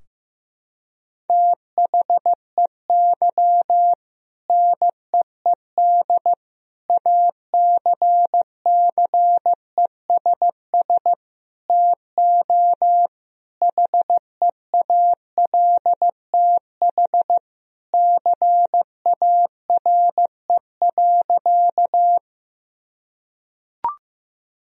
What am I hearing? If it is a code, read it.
Morse code, heard as a keyed tone, about 16 wpm, 700 Hz. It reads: THEY NEED ACCESS TO HEALTH CARE.
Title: Sentences from Top 1000 Words - Review Part 4 - 15wpm